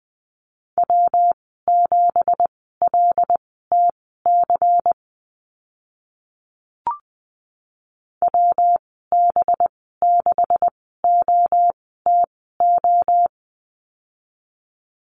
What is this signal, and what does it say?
Morse code, heard as a keyed tone, about 20 wpm, spaced as for 14 wpm, 700 Hz. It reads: W7LTC WB6OTO
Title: 186 Lesson 30g - Letter b - Call Signs 20wpm